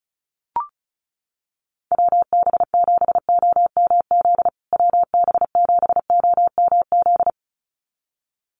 Morse code, heard as a keyed tone, about 34 wpm, 700 Hz. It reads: WB7OMZ WB7OMZ
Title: USA Call Signs Repeated 2x 35wpm